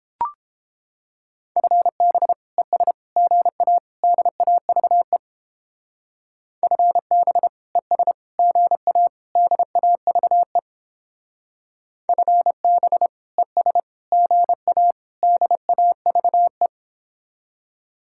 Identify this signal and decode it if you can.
Morse code, heard as a keyed tone, about 29 wpm, 700 Hz. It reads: FB ES GA DAVE FB ES GA DAVE FB ES GA DAVE